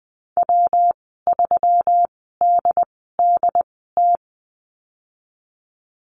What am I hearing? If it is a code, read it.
Morse code, heard as a keyed tone, about 20 wpm, spaced as for 15 wpm, 700 Hz. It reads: W3DDT
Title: 154 Lesson 25g - Letter p - Call Signs 20wpm